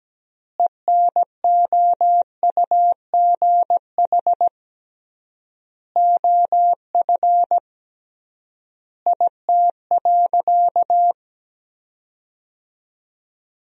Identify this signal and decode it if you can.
Morse code, heard as a keyed tone, about 17 wpm, 700 Hz. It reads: ENOUGH OF IT.
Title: 3x Word-Spacing - Sentences from Top 400 Words - 17wpm